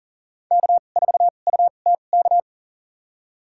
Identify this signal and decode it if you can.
Morse code, heard as a keyed tone, about 40 wpm, spaced as for 29 wpm, 700 Hz. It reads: K4UTK
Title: USA Call Signs - 5 Characters - 2x Character Spacing - 40wpm